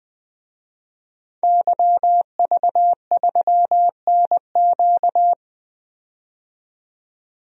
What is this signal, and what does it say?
Morse code, heard as a keyed tone, about 20 wpm, 700 Hz. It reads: YV3NQ
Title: DXCC-100 Call Signs - 5 Characters - 20wpm